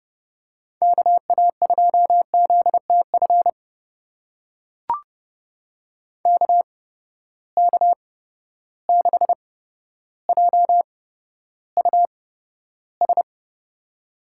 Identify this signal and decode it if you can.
Morse code, heard as a keyed tone, about 30 wpm, spaced as for 13 wpm, 700 Hz. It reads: KA2ZTF KK6JUS
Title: USA Call Signs - 6 Characters - 8x Character Spacing - 30wpm